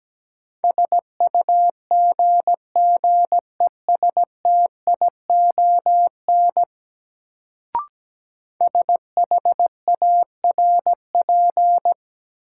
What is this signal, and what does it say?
Morse code, heard as a keyed tone, about 17 wpm, 700 Hz. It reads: SUGGESTION SHARP
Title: Sentences from Top 1600 Words - New Word Drill - 17wpm